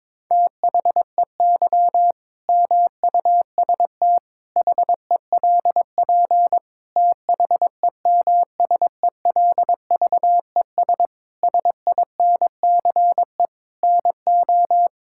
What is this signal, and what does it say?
Morse code, heard as a keyed tone, about 22 wpm, 700 Hz. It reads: THEY MUST HELP THEMSELVES SINCE NO
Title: Sentences from Top 1000 Words - Review Part 5 - 22wpm